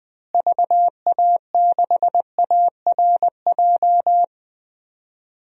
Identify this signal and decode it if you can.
Morse code, heard as a keyed tone, about 20 wpm, 700 Hz. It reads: VA6ARJ
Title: DXCC-100 Call Signs - 6 Characters - Repeated 2x - 20wpm